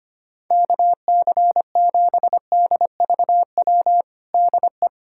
Morse code, heard as a keyed tone, about 25 wpm, 700 Hz. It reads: KC7DVW DE
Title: USA Callsign DE Callsign 25wpm